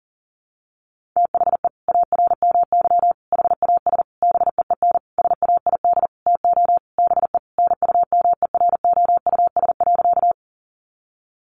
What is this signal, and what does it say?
Morse code, heard as a keyed tone, about 37 wpm, 700 Hz. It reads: THE ARMY HAS BEEN SAID TO BE NUMEROUS.
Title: Sentences from Top 1400 Words 40wpm